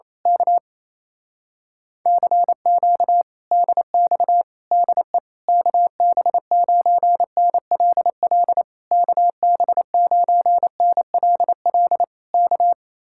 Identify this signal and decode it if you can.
Morse code, heard as a keyed tone, about 26 wpm, 700 Hz. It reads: K CQ DX DE KB9NLL KB9NLL K
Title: Speed-Racing Common QSO Phrases Encore - 28wpm